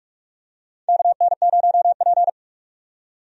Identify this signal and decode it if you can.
Morse code, heard as a keyed tone, about 44 wpm, 700 Hz. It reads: KN0P
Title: USA Call Signs - 4 Characters - 45wpm